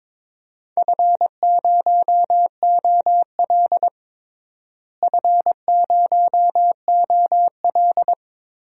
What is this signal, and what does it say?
Morse code, heard as a keyed tone, about 22 wpm, 700 Hz. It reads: F0OL F0OL
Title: DXCC-100 Call Signs - 4 Characters - Repeated 3x - 22wpm